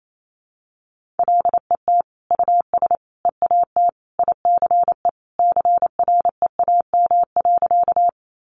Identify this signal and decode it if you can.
Morse code, heard as a keyed tone, about 28 wpm, 700 Hz. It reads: LET US EAT ICE CREAM.
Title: Sentences from Top 1500 Words 28wpm